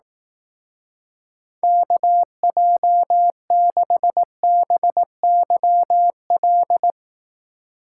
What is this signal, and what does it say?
Morse code, heard as a keyed tone, about 18 wpm, 700 Hz. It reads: KJ6BYL
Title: Speed Racing - USA Call Signs 15wpm